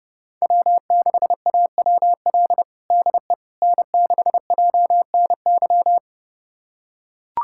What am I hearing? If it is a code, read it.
Morse code, heard as a keyed tone, about 32 wpm, 700 Hz. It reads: W6AWL DE N6JNY
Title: USA Callsign DE Callsign - 5 Characters - 30wpm